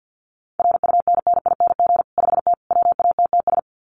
Morse code, heard as a keyed tone, about 50 wpm, 700 Hz. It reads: RUNNING 5T WATTS
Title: Common QSO Phrases 50wpm